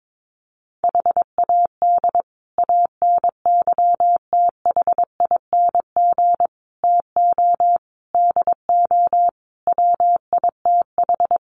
Morse code, heard as a keyed tone, about 22 wpm, 700 Hz. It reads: HAD ANYTHING TO DO WITH
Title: Sets of 5 Words Encore 22wpm